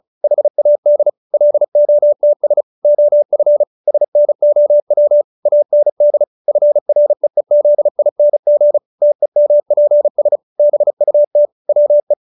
Morse code, heard as a keyed tone, about 34 wpm, 590 Hz. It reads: HAD LOTS OF SNOW AND FREEZING TEMPS BUT WE
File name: QSO312 35